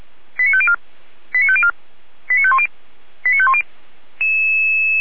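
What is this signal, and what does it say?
Five-tone selective call (ZVEI), heard as quick runs of five tones, tones of about 70 ms. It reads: 89504 89504 89510 89510
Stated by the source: ZVEI five-tone call 89160 received by SDR, logged by Feuerwehr Orlamünde fire brigade.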